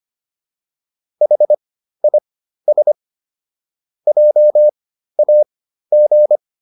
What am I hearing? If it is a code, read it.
Morse code, heard as a keyed tone, about 25 wpm, spaced as for 13 wpm, 600 Hz. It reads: HIS JAG